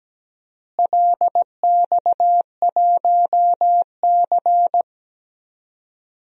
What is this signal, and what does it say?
Morse code, heard as a keyed tone, about 17 wpm, 700 Hz. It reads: LX1C
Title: DXCC-100 Call Signs - 4 Characters - Repeated 2x - 17wpm